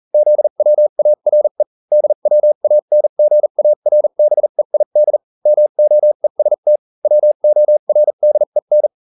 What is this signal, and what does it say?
Morse code, heard as a keyed tone, about 40 wpm, spaced as for 35 wpm, 600 Hz. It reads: ZWARE DWANGARBEID MOEST WORDEN